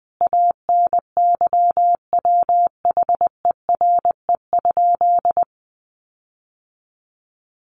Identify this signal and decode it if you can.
Morse code, heard as a keyed tone, about 20 wpm, 700 Hz. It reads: ANYWHERE?
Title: Sentences from Top 1400 Words 20wpm